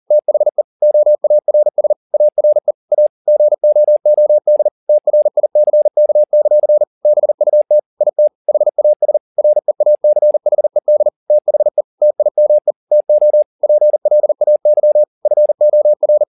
Morse code, heard as a keyed tone, about 40 wpm, 600 Hz. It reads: THE OARS ARE A GOOD TRICK; BUT IT HAS REACHED THE TIME TO PLAY FOR
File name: the old man and the sea 2 40wpm